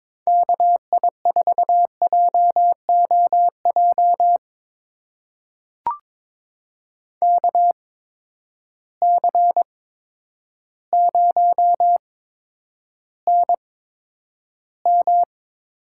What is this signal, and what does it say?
Morse code, heard as a keyed tone, about 22 wpm, spaced as for 10 wpm, 700 Hz. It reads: KI4JOJ KC0NM
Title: USA Call Signs - 6 Characters - 8x Character Spacing - 22wpm